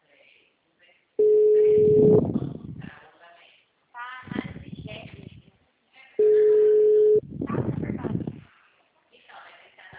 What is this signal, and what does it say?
Ringback tone (Europe, 425 Hz), with speech.